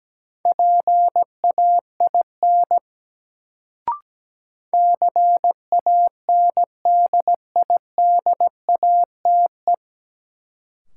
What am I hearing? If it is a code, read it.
Morse code, heard as a keyed tone, about 17 wpm, 700 Hz. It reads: PAIN CANDIDATE
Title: Sentences from Top 1600 Words - New Word Drill Alt - 17wpm